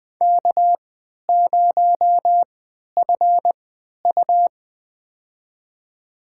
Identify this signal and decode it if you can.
Morse code, heard as a keyed tone, about 20 wpm, spaced as for 11 wpm, 700 Hz. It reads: K0FU
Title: USA Call Signs - 3x Character Spacing - 20wpm